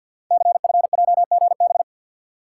Morse code, heard as a keyed tone, about 50 wpm, 700 Hz. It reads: CFJGB
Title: Code Groups - 5 Letters - 50wpm